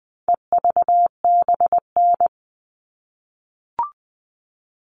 Morse code, heard as a keyed tone, about 20 wpm, 700 Hz. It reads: EVBN
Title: All Cut Numbers - 4-Digit 20wpm